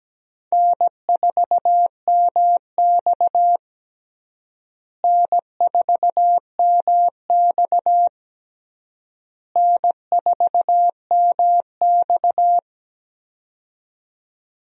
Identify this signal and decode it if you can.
Morse code, heard as a keyed tone, about 17 wpm, 700 Hz. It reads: N4MX N4MX N4MX
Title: USA Call Signs - 4 Characters - Repeated 3x - 17wpm